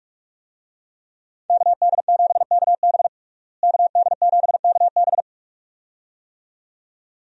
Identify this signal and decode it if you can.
Morse code, heard as a keyed tone, about 46 wpm, 700 Hz. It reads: KD7KB KD7KB
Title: USA Call Signs - 5 Characters - Repeated 2x - 45wpm